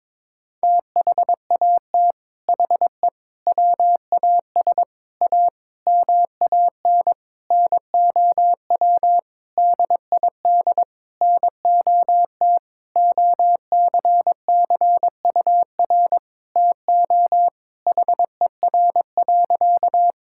Morse code, heard as a keyed tone, about 22 wpm, 700 Hz. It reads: THAT HE WAS A MAN NOW DID NOT OCCUR TO HER.
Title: Sentences from Top 1000 Words - Review Part 6 - 22wpm